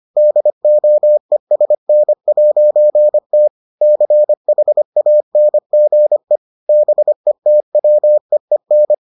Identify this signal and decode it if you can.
Morse code, heard as a keyed tone, about 25 wpm, 600 Hz. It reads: DOESN'T CHANGE BETWEEN